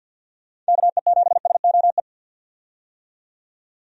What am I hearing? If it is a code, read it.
Morse code, heard as a keyed tone, about 47 wpm, 700 Hz. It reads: KE7SOE
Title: USA Call Signs - 6 Characters - 50wpm